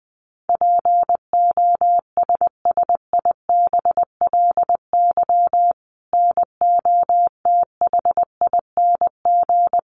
Morse code, heard as a keyed tone, about 19 wpm, 700 Hz. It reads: POSSIBLY NOTHING
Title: Sentences from Top 1300 Words 20wpm